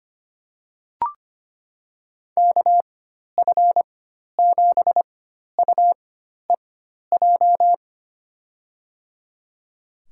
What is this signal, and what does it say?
Morse code, heard as a keyed tone, about 25 wpm, spaced as for 12 wpm, 700 Hz. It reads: KF7UEJ
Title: USA Call Signs - 6 Characters - 4x Character Spacing - 25wpm